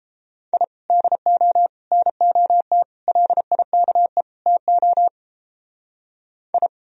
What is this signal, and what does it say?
Morse code, heard as a keyed tone, about 29 wpm, 700 Hz. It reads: I DO NOT LIKE TO I